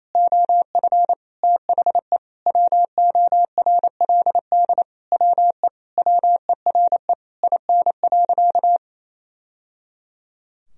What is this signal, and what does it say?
Morse code, heard as a keyed tone, about 28 wpm, 700 Hz. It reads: OF THE WORLD WE WERE IN.
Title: Sentences from Top 1000 Words - Review Part 3 - 28wpm